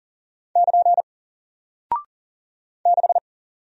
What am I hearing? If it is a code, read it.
Morse code, heard as a keyed tone, about 40 wpm, 700 Hz. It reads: <KN> 6
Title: Single Character (Letter:Number:Punctuation:Prosign) Alt 40wpm